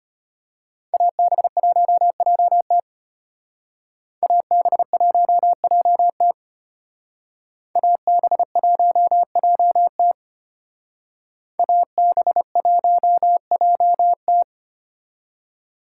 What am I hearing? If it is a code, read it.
Morse code, heard as a keyed tone, about 32 wpm, 700 Hz. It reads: AB1JT AB1JT AB1JT AB1JT